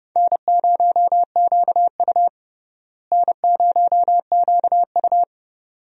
Morse code, heard as a keyed tone, about 30 wpm, 700 Hz. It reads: N0QU N0QU
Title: USA Call Signs - 4 Characters - Repeated 2x - 30wpm